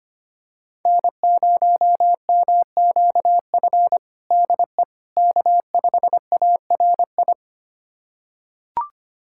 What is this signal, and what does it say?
Morse code, heard as a keyed tone, about 25 wpm, 700 Hz. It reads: N0MQF DE K5ARI
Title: USA Callsign DE Callsign 25wpm